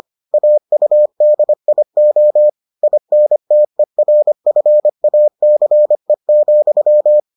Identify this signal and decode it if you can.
Morse code, heard as a keyed tone, about 25 wpm, 600 Hz. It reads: AUDIO INTERFACE,